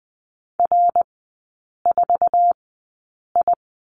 Morse code, heard as a keyed tone, about 20 wpm, spaced as for 8 wpm, 700 Hz. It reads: R4I
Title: 074 Lesson 13e - Letter L - Three Characters 20wpm